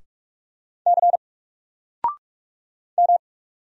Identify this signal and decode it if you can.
Morse code, heard as a keyed tone, about 44 wpm, 700 Hz. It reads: C M
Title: Single Character (Letter:Number:Punctuation:Prosign) Alt 45wpm